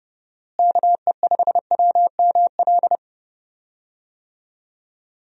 Morse code, heard as a keyed tone, about 32 wpm, 700 Hz. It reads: KE5WML
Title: USA Call Signs - 6 Characters - Repeated 2x - 30wpm